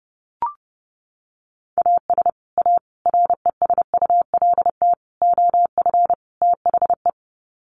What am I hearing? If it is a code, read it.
Morse code, heard as a keyed tone, about 30 wpm, 700 Hz. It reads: AS A RESULT OF THE
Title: Sets of 5 Words 30wpm